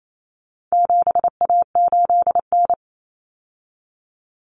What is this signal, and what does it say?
Morse code, heard as a keyed tone, about 28 wpm, 700 Hz. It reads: 7A8N
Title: Common Cut Numbers - 4-Digit 28wpm